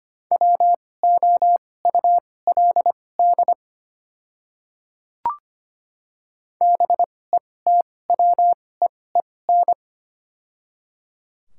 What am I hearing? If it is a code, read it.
Morse code, heard as a keyed tone, about 25 wpm, spaced as for 18 wpm, 700 Hz. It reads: WOULD BETWEEN
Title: 2x Character Spacing - Top 300 Words - 25wpm